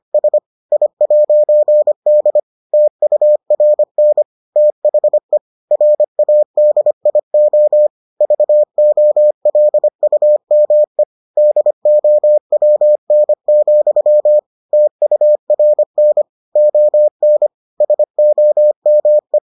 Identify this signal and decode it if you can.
Morse code, heard as a keyed tone, about 25 wpm, 600 Hz. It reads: S I'D TURN THE RADIO VOLUME DOWN, TURN ON SOME